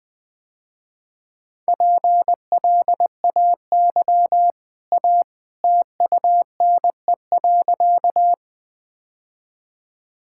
Morse code, heard as a keyed tone, about 20 wpm, 700 Hz. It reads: PLAY A TUNE.
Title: Easy Sentences with Hints - 20wpm